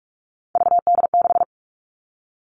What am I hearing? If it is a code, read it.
Morse code, heard as a keyed tone, about 46 wpm, 700 Hz. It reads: VD6
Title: Three Challenging Letter-Number Combinations 45wpm